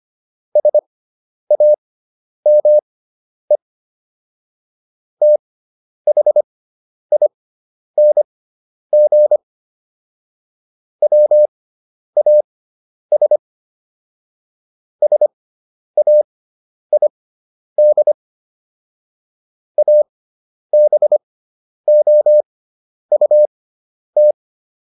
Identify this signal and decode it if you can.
Morse code, heard as a keyed tone, about 25 wpm, spaced as for 10 wpm, 600 Hz. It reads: SAME THING WAS SAID ABOUT
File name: SS213 10